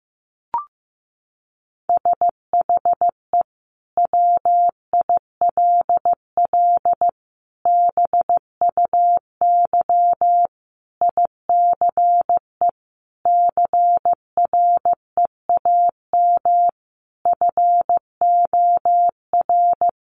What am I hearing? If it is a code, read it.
Morse code, heard as a keyed tone, about 15 wpm, 700 Hz. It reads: SHE WILL BUY ICE CREAM FOR
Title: Sentences from Top 1500 Words 15wpm